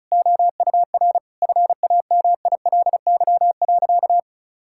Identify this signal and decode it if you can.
Morse code, heard as a keyed tone, about 34 wpm, 700 Hz. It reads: OUR FAMILY.
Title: Sentences from Top 1000 Words - Review Part 6 - 35wpm